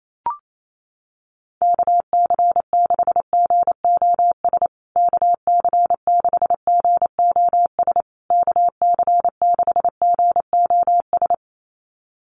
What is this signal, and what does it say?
Morse code, heard as a keyed tone, about 28 wpm, 700 Hz. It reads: KC6GOS KC6GOS KC6GOS
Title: USA Call Signs Repeated 3x 28wpm